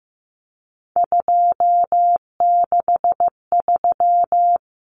Morse code, heard as a keyed tone, about 15 wpm, 700 Hz. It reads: 263
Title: Numbers - 3-Digit 15wpm